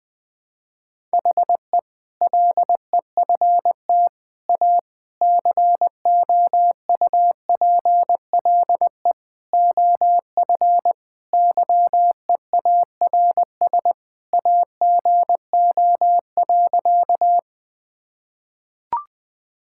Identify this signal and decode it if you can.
Morse code, heard as a keyed tone, about 20 wpm, 700 Hz. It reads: HE LEFT A COUPLE OF YEARS AGO.
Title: Sentences with 4-Word Phrases - 20wpm